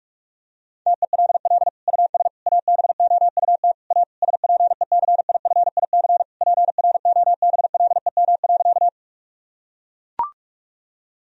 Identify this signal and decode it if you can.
Morse code, heard as a keyed tone, about 41 wpm, 700 Hz. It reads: TELL US ABOUT A SPECIFIC PROBLEM.
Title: Sentences from Top 1000 Words - Review Part 3 - 45wpm